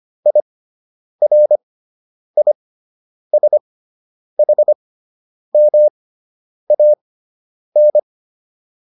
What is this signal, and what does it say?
Morse code, heard as a keyed tone, about 25 wpm, spaced as for 9 wpm, 600 Hz. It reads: IRISHMAN